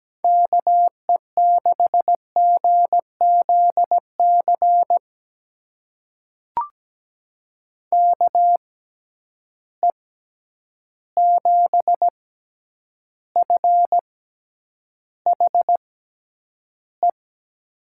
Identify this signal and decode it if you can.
Morse code, heard as a keyed tone, about 17 wpm, spaced as for 9 wpm, 700 Hz. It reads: KE6GZC KE7FHE
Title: USA Call Signs - 6 Characters - 6x Character Spacing - 17wpm